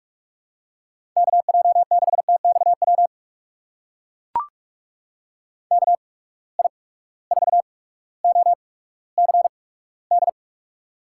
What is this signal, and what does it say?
Morse code, heard as a keyed tone, about 45 wpm, 700 Hz. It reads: K J 6 T X W K I 4 O C D
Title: USA Call Signs - 6 Characters - 8x Character Spacing - 45wpm